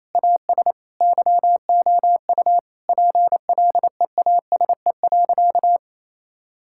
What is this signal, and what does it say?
Morse code, heard as a keyed tone, about 27 wpm, 700 Hz. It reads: AS YOU PLEASE.
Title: Sentences from Top 1600 Words 28wpm